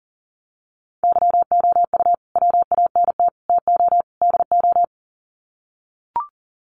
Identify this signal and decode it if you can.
Morse code, heard as a keyed tone, about 40 wpm, 700 Hz. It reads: YOU WANT TO DO
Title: Sets of 4 Words 40wpm